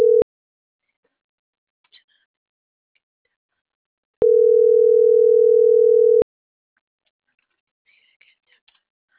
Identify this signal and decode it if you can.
Ringback tone (North America), 440 + 480 Hz.